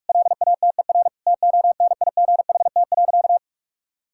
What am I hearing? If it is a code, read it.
Morse code, heard as a keyed tone, about 46 wpm, 700 Hz. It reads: LATER TONIGHT.
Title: Sentences from Top 1000 Words - Review Part 6 - 45wpm